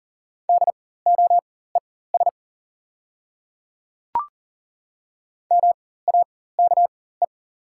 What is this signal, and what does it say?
Morse code, heard as a keyed tone, about 40 wpm, spaced as for 19 wpm, 700 Hz. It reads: DOES MAKE
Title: 4x Character Spacing - Top 200 Words - 40wpm